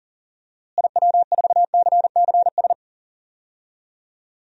Morse code, heard as a keyed tone, about 40 wpm, 700 Hz. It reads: IW4CCS
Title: DXCC-100 Call Signs - 6 Characters - Repeated 3x - 40wpm